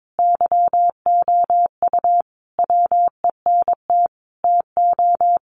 Morse code, heard as a keyed tone, about 22 wpm, 700 Hz. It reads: YOU WENT TO
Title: Sentences from Top 400 Words 22wpm